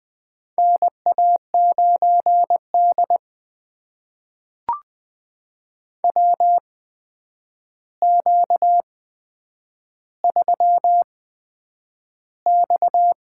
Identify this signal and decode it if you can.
Morse code, heard as a keyed tone, about 20 wpm, spaced as for 9 wpm, 700 Hz. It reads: NA9D WQ3X
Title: USA Call Signs - 8x Character Spacing - 20wpm